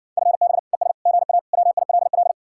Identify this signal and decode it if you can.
Morse code, heard as a keyed tone, about 60 wpm, 700 Hz. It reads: FB ES GA WILL